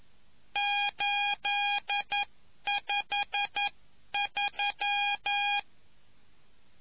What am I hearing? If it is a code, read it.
Morse code, heard as a keyed tone, about 11 wpm, spaced as for 9 wpm, 820 Hz. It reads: 853